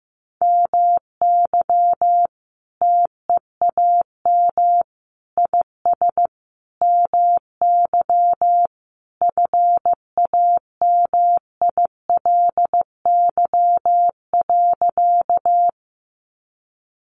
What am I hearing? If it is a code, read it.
Morse code, heard as a keyed tone, about 15 wpm, 700 Hz. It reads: MY TEAM IS MY FAMILY.
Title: Sentences from Top 1500 Words 15wpm